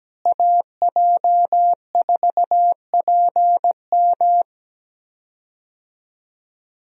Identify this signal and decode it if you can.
Morse code, heard as a keyed tone, about 17 wpm, 700 Hz. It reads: AJ4PM
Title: USA Call Signs - 5 Characters - Repeated 3x - 17wpm